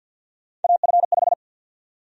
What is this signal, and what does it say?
Morse code, heard as a keyed tone, about 53 wpm, 700 Hz. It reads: AF5
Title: USA Call Sign Prefixes - 50wpm